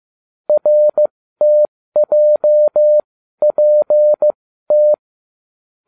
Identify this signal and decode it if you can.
Morse code, heard as a keyed tone, about 15 wpm, spaced as for 12 wpm, 600 Hz. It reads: RTJPT